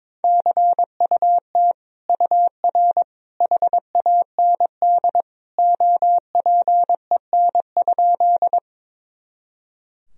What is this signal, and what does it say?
Morse code, heard as a keyed tone, about 22 wpm, 700 Hz. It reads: CUT UR HAND OPEN?
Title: Sentences from Top 300 Words and Abbreviations 22wpm